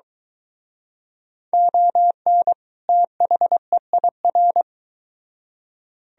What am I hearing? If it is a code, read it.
Morse code, heard as a keyed tone, about 23 wpm, 700 Hz. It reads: ON THEIR